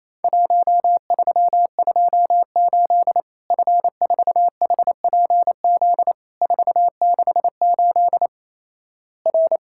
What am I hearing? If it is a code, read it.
Morse code, heard as a keyed tone, about 28 wpm, 700 Hz. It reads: 1328 F4HPZ 468 R
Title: Worked All Europe DX Contest - QTC Exchange - 28wpm